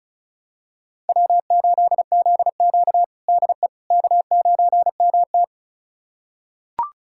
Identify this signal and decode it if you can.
Morse code, heard as a keyed tone, about 36 wpm, 700 Hz. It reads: W8ZQ DE K9MT
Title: USA Callsign DE Callsign - 4 Characters - 35wpm